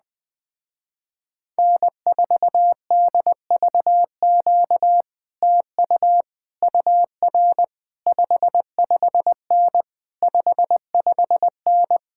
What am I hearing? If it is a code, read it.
Morse code, heard as a keyed tone, about 21 wpm, 700 Hz. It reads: N4DVQ TU UR 55N 55N